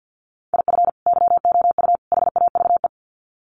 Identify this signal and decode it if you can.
Morse code, heard as a keyed tone, about 50 wpm, 700 Hz. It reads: IF YOU HAVE
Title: Sets of 3 Words 50wpm